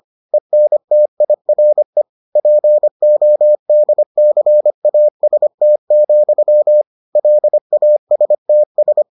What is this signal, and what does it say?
Morse code, heard as a keyed tone, about 25 wpm, 600 Hz. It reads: ENTIRE PODCAST, LASTS